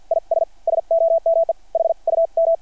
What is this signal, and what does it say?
Morse code, heard as a keyed tone, about 50 wpm, spaced as for 42 wpm, 660 Hz. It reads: IS SOME HUM
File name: archived / CRSnet